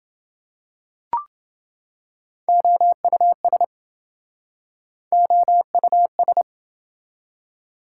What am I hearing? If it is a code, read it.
Morse code, heard as a keyed tone, about 27 wpm, 700 Hz. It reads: OUS OUS